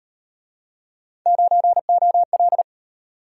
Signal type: Morse code

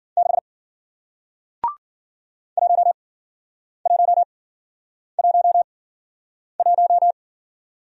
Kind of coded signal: Morse code

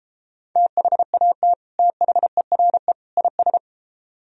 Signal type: Morse code

